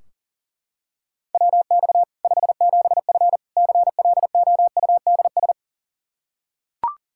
Morse code, heard as a keyed tone, about 37 wpm, 700 Hz. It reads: WX 57F CLOUDS
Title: K1USN SST QSO Practice 40wpm